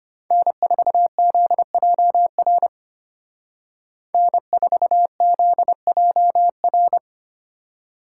Morse code, heard as a keyed tone, about 32 wpm, 700 Hz. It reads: N4ZJR N4ZJR